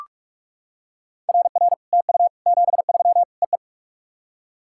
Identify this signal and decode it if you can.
Morse code, heard as a keyed tone, about 42 wpm, 690 Hz. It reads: RR TU 73 EE
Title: SOTA QSO Practice 45wpm